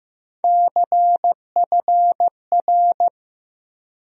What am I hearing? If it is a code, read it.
Morse code, heard as a keyed tone, about 15 wpm, 700 Hz. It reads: CFR
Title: USA Call Sign Postfixes - 3 Letters 15wpm